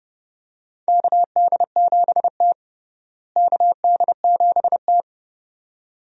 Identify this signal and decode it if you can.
Morse code, heard as a keyed tone, about 30 wpm, 700 Hz. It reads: KD7T KD7T
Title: USA Call Signs - 4 Characters - Repeated 2x - 30wpm